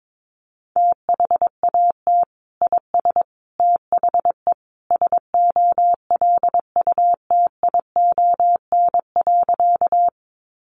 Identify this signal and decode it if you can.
Morse code, heard as a keyed tone, about 22 wpm, 700 Hz. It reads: THAT IS THE SOLUTION.
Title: Sentences from Top 1300 Words 22wpm